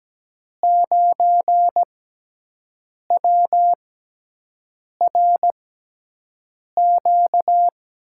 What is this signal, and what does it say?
Morse code, heard as a keyed tone, about 17 wpm, 700 Hz. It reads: 9 W R Q
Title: USA Call Signs - 5 Characters - 6x Character Spacing - 17wpm